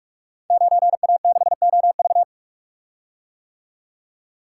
Morse code, heard as a keyed tone, about 44 wpm, 700 Hz. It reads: 9A6OV
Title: Code Groups - 5 Letter:Numbers - 45wpm